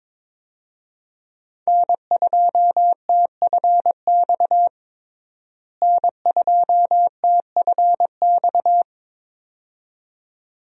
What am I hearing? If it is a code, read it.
Morse code, heard as a keyed tone, about 22 wpm, 700 Hz. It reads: N2TFX N2TFX
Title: USA Call Signs - 5 Characters - Repeated 2x - 22wpm